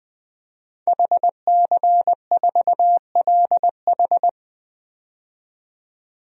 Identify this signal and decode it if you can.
Morse code, heard as a keyed tone, about 20 wpm, 700 Hz. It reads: HC4LH
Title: DXCC-100 Call Signs - 5 Characters - 20wpm